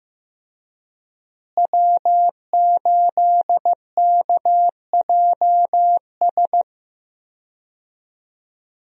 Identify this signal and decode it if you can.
Morse code, heard as a keyed tone, about 15 wpm, 700 Hz. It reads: W8KJS
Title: USA Call Signs - 5 Characters - Repeated 3x - 15wpm